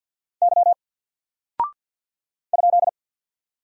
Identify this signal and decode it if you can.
Morse code, heard as a keyed tone, about 48 wpm, 700 Hz. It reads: Y ?
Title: Single Character (Letter:Number:Punctuation:Prosign) 50wpm